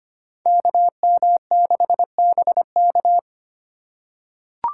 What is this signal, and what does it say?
Morse code, heard as a keyed tone, about 25 wpm, 700 Hz. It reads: KM6BK
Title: USA Call Signs - 5 Characters - 2x Character Spacing - 25wpm